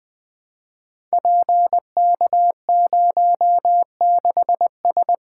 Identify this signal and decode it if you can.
Morse code, heard as a keyed tone, about 20 wpm, 700 Hz. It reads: PK06S
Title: Code Groups - 5 Letter:Numbers - 20wpm